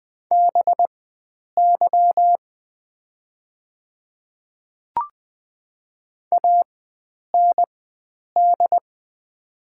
Morse code, heard as a keyed tone, about 20 wpm, spaced as for 9 wpm, 700 Hz. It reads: BY AND
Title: 4x Character Spacing - Top 100 Words - 20wpm